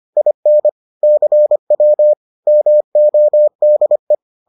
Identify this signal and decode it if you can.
Morse code, heard as a keyed tone, about 25 wpm, 600 Hz. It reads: IN CW MODE